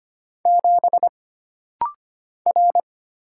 Morse code, heard as a keyed tone, about 25 wpm, 700 Hz. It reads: 7 R
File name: Single Character (Letter:Number:Punctuation:Prosign) - Rapid-Fire - 25wpm